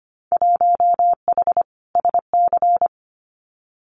Morse code, heard as a keyed tone, about 25 wpm, 700 Hz. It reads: 1H SC
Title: Winter Field Day Exchanges 25wpm